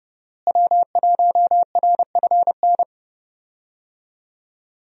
Morse code, heard as a keyed tone, about 30 wpm, 700 Hz. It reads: W1RFN
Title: USA Call Signs - 5 Characters - Repeated 3x - 30wpm